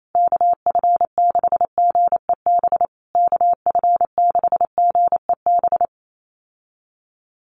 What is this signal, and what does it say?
Morse code, heard as a keyed tone, about 28 wpm, 700 Hz. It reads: KF6GEB KF6GEB
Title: USA Call Signs Repeated 3x 28wpm